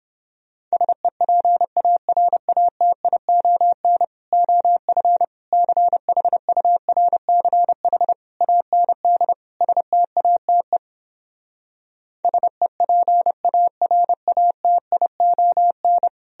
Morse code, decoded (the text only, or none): SEPARATION OF CHURCH AND STATE SEPARATION